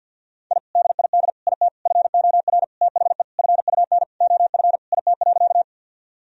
Morse code, heard as a keyed tone, about 50 wpm, 700 Hz. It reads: I DID IT FOR THE FUN OF IT.
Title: Sentences from Top 1000 Words - Review Part 4 - 50wpm